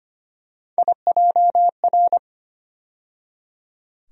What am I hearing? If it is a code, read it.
Morse code, heard as a keyed tone, about 25 wpm, 700 Hz. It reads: IJR
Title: USA Call Sign Postfixes - 3 Letters 25wpm